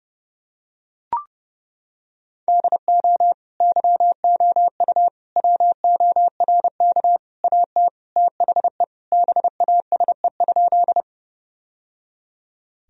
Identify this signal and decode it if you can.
Morse code, heard as a keyed tone, about 30 wpm, 700 Hz. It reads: DO YOU WORK AT THE BASE?
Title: Sentences from Top 1000 Words - Review Part 1 - 30wpm